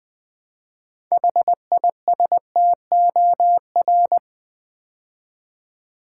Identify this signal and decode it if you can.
Morse code, heard as a keyed tone, about 20 wpm, 700 Hz. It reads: HISTOR
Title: Build A Word - 3 Letters At A Time - 8-Letter Words - 20wpm